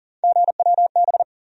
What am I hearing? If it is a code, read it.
Morse code, heard as a keyed tone, about 37 wpm, 700 Hz. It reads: GWB
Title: USA Call Sign Postfixes - 3 Letters 40wpm